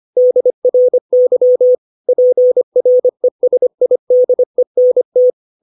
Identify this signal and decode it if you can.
Morse code, heard as a keyed tone, about 25 wpm, 500 Hz. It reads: DRY PRESIDENT
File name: 632 words / 632 Words 25WPM